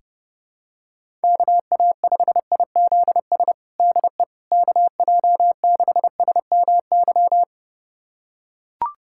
Morse code, heard as a keyed tone, about 30 wpm, 700 Hz. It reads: KA5IZS DE KJ6SMY
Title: USA Callsign DE Callsign - 6 Characters - 30wpm